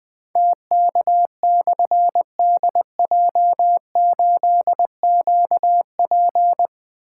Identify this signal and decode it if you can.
Morse code, heard as a keyed tone, about 20 wpm, 700 Hz. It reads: TK/DJ8QP
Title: Difficult Call Signs 20wpm